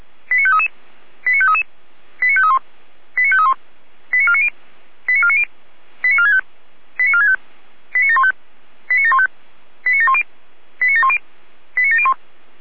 Five-tone selective call (ZVEI), heard as quick runs of five tones, tones of about 70 ms. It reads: 89533 89533 89531 89531 89490 89490 89465 89465 89715 89715 89710 89710 89701